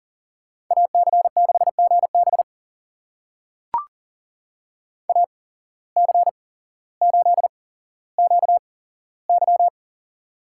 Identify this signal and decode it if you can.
Morse code, heard as a keyed tone, about 40 wpm, spaced as for 17 wpm, 700 Hz. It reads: AC6GB AC8QY
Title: USA Call Signs - 5 Characters - 8x Character Spacing - 40wpm